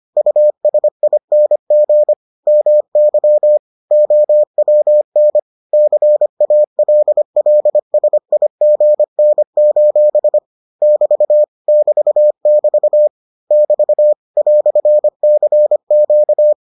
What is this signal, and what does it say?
Morse code, heard as a keyed tone, about 25 wpm, 600 Hz. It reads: USING MY OWN CALLSIGN: <BT> <BT><BT> <BT> "CQ